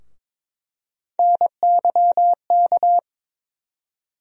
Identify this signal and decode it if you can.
Morse code, heard as a keyed tone, about 21 wpm, 700 Hz. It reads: NYK
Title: USA Call Sign Postfixes - 3 Letters 22wpm